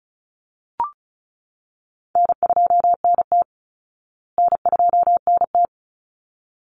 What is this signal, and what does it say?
Morse code, heard as a keyed tone, about 34 wpm, 700 Hz. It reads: N2NT N2NT
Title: Easy Call Signs Repeated 2x - 35wpm